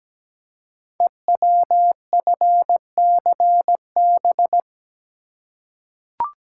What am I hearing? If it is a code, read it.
Morse code, heard as a keyed tone, about 17 wpm, 700 Hz. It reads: EWFCB
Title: Code Groups - 5 Letters - 17wpm